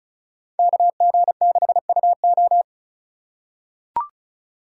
Morse code, heard as a keyed tone, about 36 wpm, 700 Hz. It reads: KG6UO
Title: USA Call Signs - 3x Character Spacing - 35wpm